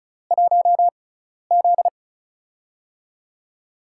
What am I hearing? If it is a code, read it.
Morse code, heard as a keyed tone, about 36 wpm, spaced as for 12 wpm, 700 Hz. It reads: 1Z
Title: USA Call Signs - 4 Characters - 6x Character Spacing - 35wpm